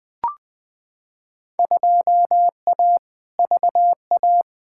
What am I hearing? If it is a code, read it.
Morse code, heard as a keyed tone, about 20 wpm, 700 Hz. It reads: 2A VA
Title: Field Day Exchanges - 20wpm